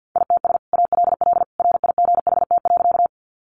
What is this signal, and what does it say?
Morse code, heard as a keyed tone, about 50 wpm, 700 Hz. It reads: ITS ALL RIGHT.
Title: Sentences from Top 600 Words 50wpm